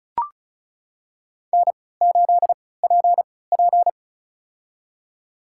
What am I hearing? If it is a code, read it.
Morse code, heard as a keyed tone, about 34 wpm, spaced as for 20 wpm, 700 Hz. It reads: N8PP
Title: USA Call Signs - 4 Characters - 3x Character Spacing - 35wpm